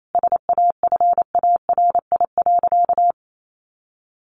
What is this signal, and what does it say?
Morse code, heard as a keyed tone, about 28 wpm, 700 Hz. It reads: SAFARI.
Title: A Sound Of Thunder 28wpm